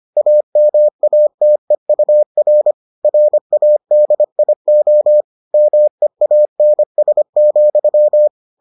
Morse code, heard as a keyed tone, about 25 wpm, 600 Hz. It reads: AMATEUR RADIO MEANS,